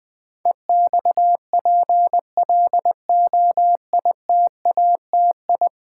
Morse code, heard as a keyed tone, about 20 wpm, 700 Hz. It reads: EXPLOITATI
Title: Build A Word - 5 Letters At A Time - 12-Letter Words - 20wpm